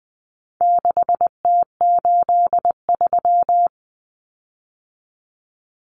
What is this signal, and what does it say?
Morse code, heard as a keyed tone, about 20 wpm, 700 Hz. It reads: 6T83
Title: Common Cut Numbers - 4-Digit 20wpm